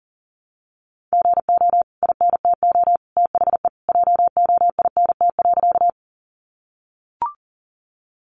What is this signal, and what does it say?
Morse code, heard as a keyed tone, about 40 wpm, 700 Hz. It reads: GO INTO THE JOINT.
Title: Sentences from Top 1300 Words 40wpm